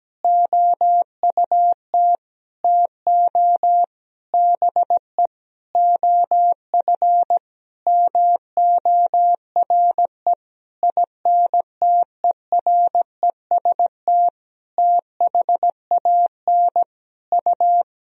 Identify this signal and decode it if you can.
Morse code, heard as a keyed tone, about 17 wpm, 700 Hz. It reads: OUT TO BE OF MORE INTEREST THAN U
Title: Sentences from Top 400 Words and Abbreviations 17wpm